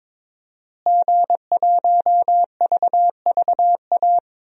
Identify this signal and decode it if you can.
Morse code, heard as a keyed tone, about 22 wpm, 700 Hz. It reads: G1VVA
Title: DXCC-100 Call Signs - 5 Characters - Repeated 2x - 22wpm